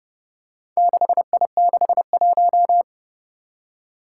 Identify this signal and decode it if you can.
Morse code, heard as a keyed tone, about 30 wpm, 700 Hz. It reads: 6I61
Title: Four Challenging Letter-Number Combinations 30wpm